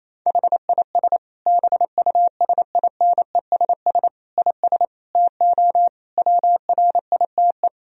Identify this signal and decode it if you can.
Morse code, heard as a keyed tone, about 28 wpm, 700 Hz. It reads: HIS BUSINESS IS TO WRITE
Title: Sentences from Top 1000 Words - Review Part 3 - 28wpm